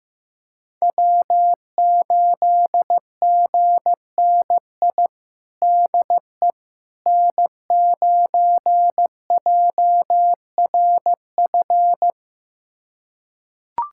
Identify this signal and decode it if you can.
Morse code, heard as a keyed tone, about 15 wpm, 700 Hz. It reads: W8GNI DE N9JRF
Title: USA Callsign DE Callsign - 5 Characters - 15wpm